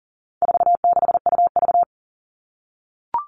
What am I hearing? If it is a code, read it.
Morse code, heard as a keyed tone, about 40 wpm, 700 Hz. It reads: 46UV
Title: All Cut Numbers - 4-Digit 40wpm